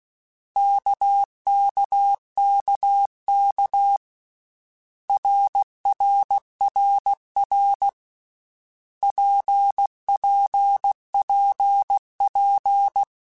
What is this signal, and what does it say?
Morse code, heard as a keyed tone, about 16 wpm, 800 Hz. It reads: KKKK RRRR PPPP
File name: Basic Letter Groups By7Groups / RKPX / RKPX@13